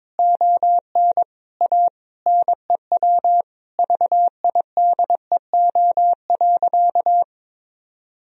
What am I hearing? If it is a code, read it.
Morse code, heard as a keyed tone, about 22 wpm, 700 Hz. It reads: ON A NEW VIDEO.
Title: Sentences from Top 1000 Words - Review Part 4 - 22wpm